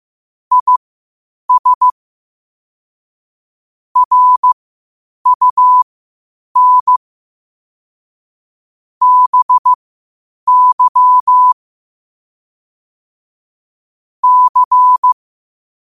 Morse code, heard as a keyed tone, about 15 wpm, spaced as for 8 wpm, 1000 Hz. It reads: IS RUN BY C
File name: qotdfiles / 8wpm